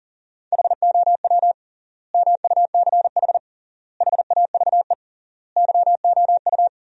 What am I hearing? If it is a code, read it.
Morse code, heard as a keyed tone, about 37 wpm, 700 Hz. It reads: HOW MUCH HAVE YOU
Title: 3x Word-Spacing - Sentences from Top 300 Words - 40wpm